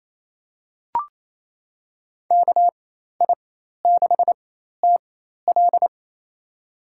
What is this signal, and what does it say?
Morse code, heard as a keyed tone, about 28 wpm, spaced as for 13 wpm, 700 Hz. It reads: KI6TL
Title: USA Call Signs - 5 Characters - 4x Character Spacing - 28wpm